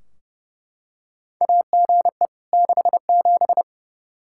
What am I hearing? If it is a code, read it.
Morse code, heard as a keyed tone, about 32 wpm, 700 Hz. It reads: AGE 67
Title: Practice QSO - Exchange 3 - More Info - 30wpm